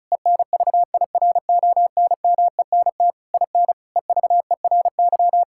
Morse code, heard as a keyed tone, about 36 wpm, 700 Hz. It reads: ENVIRONMENT IN EVERY
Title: Sentences from Top 1000 Words - Review Part 5 - 35wpm